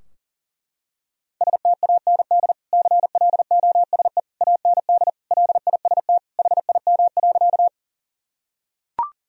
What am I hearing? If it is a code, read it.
Morse code, heard as a keyed tone, about 37 wpm, 700 Hz. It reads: STAND CLOSE AND LIST HIM.
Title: Sentences from Top 1000 Words - Review Part 2 - 40wpm